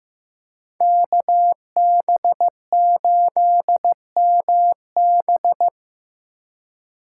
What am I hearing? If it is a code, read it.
Morse code, heard as a keyed tone, about 18 wpm, 700 Hz. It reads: KB8MB
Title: Speed Racing - USA 2x2 Call Signs 15wpm